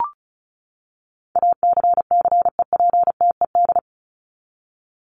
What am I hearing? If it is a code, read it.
Morse code, heard as a keyed tone, about 34 wpm, 700 Hz. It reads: ACCEPTED
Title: ED Suffix - Top 100 Words with Base Words 35wpm